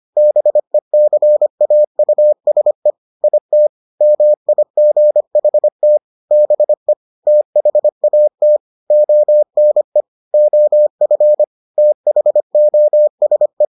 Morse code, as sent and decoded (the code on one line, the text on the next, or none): -... -.-. .- ..- ... / .. - / -- .. --. .... - / -... / - .... .- - / --- -. / --- ..-. / - .... --- ... .
BECAUSE IT MIGHT BE THAT ONE OF THOSE